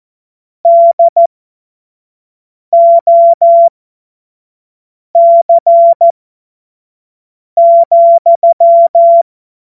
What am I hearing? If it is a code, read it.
Morse code, heard as a keyed tone, about 14 wpm, spaced as for 5 wpm, 680 Hz. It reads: DOC,
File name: Morsekurs / Witze / Witz 05 März 2024 25 F70